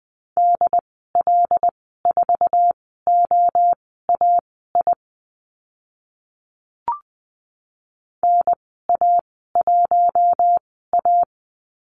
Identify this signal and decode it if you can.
Morse code, heard as a keyed tone, about 20 wpm, spaced as for 14 wpm, 700 Hz. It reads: DL4OAI NA1A
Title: 076 Lesson 13g - Letter L - Call Signs 20wpm